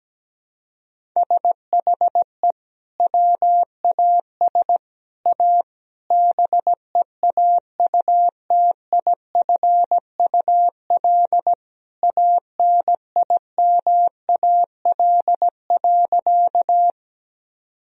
Morse code, heard as a keyed tone, about 17 wpm, 700 Hz. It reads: SHE WAS A BEAUTIFUL ANIMAL.
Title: Sentences from Top 1000 Words - Review Part 1 - 17wpm